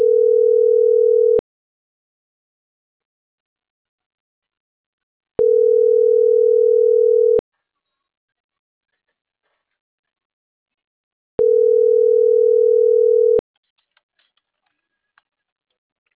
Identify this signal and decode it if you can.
Ringback tone (North America), 440 + 480 Hz.